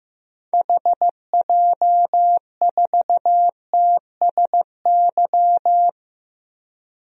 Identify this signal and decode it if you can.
Morse code, heard as a keyed tone, about 15 wpm, 700 Hz. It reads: HJ4TSY
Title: DXCC-100 Call Signs - 6 Characters - Repeated 3x - 15wpm